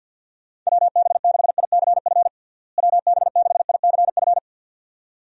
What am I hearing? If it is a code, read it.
Morse code, heard as a keyed tone, about 50 wpm, 700 Hz. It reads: WB6ICF WB6ICF
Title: USA Call Signs - 6 Characters - Repeated 2x - 50wpm